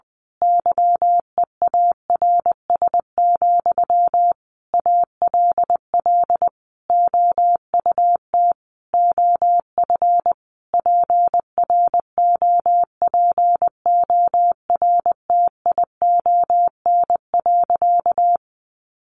Morse code, heard as a keyed tone, about 20 wpm, 700 Hz. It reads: YEARS, ALL OUT OF PROPORTION.
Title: A Sound Of Thunder 20wpm